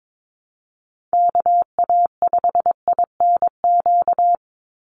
Morse code, heard as a keyed tone, about 22 wpm, 700 Hz. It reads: KA5INQ
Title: USA Call Signs 22wpm